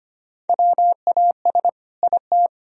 Morse code, heard as a keyed tone, about 25 wpm, 700 Hz. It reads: WAS IT
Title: Sentences from Top 1000 Words - Review Part 2 - 25wpm